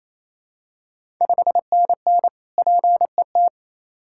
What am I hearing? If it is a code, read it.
Morse code, heard as a keyed tone, about 29 wpm, 690 Hz. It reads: 5NN PET
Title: Ontario QSO Party Exchange 28wpm